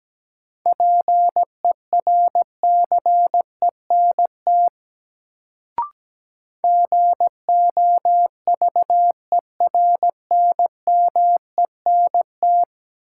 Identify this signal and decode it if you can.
Morse code, heard as a keyed tone, about 17 wpm, 700 Hz. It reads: PERCENT GOVERNMENT
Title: Sets of 2 Words - New Word Drill - 17wpm